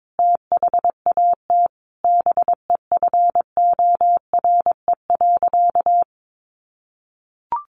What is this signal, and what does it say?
Morse code, heard as a keyed tone, about 22 wpm, 700 Hz. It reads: THAT BEFORE.
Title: Sentences from Top 400 Words 22wpm